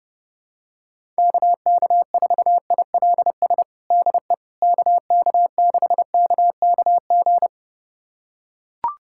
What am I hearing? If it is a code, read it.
Morse code, heard as a keyed tone, about 32 wpm, 700 Hz. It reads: KK4ILS DE KK6KKG
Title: USA Callsign DE Callsign - 6 Characters - 30wpm